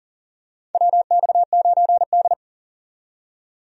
Morse code, heard as a keyed tone, about 37 wpm, 700 Hz. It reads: WX9D
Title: USA Call Signs - 4 Characters - 40wpm